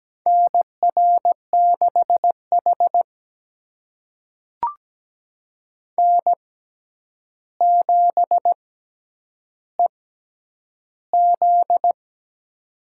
Morse code, heard as a keyed tone, about 17 wpm, spaced as for 9 wpm, 700 Hz. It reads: NR6H N7EZ
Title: USA Call Signs - 4 Characters - 6x Character Spacing - 17wpm